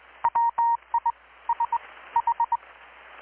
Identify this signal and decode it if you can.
Morse code, heard as a keyed tone, about 20 wpm, spaced as for 15 wpm, 930 Hz. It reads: WISH
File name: blog / swl / 6-abril-2019-12695-USB-KPS